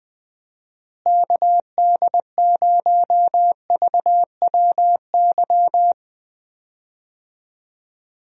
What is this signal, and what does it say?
Morse code, heard as a keyed tone, about 20 wpm, 700 Hz. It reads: KD0VWY
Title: USA Call Signs - 6 Characters - Repeated 2x - 20wpm